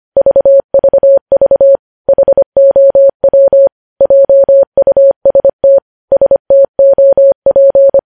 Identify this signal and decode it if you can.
Morse code, heard as a keyed tone, about 25 wpm, 550 Hz. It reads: VVV HOW JUST STOP
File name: podcast / News headlines-1x-25x25